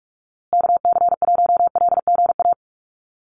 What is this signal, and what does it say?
Morse code, heard as a keyed tone, about 44 wpm, 700 Hz. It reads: KC1LGA
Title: USA Call Signs Repeated 3x 45wpm